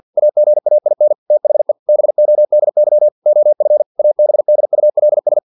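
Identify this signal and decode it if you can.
Morse code, heard as a keyed tone, about 50 wpm, 600 Hz. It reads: AGAIN THE BODY OF ABDULS
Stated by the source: Morse practice text from an English-language novel at 49 wpm.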